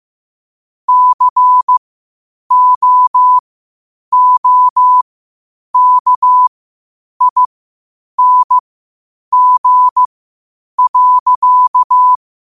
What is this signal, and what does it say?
Morse code, heard as a keyed tone, about 15 wpm, spaced as for 8 wpm, 1000 Hz. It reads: COOKING.
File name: qotdfiles / 8wpm